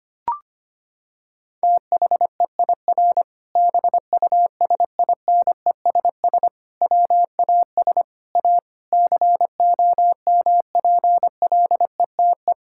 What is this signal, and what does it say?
Morse code, heard as a keyed tone, about 25 wpm, 700 Hz. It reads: THEIR BUSINESS WAS A COMPLETE
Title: Sentences from Top 1000 Words - Review Part 6 - 25wpm